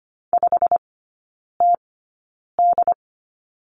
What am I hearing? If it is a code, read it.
Morse code, heard as a keyed tone, about 25 wpm, spaced as for 9 wpm, 690 Hz. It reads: 5TD
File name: Lesson 15e - Number 5 - Three Characters 25wpm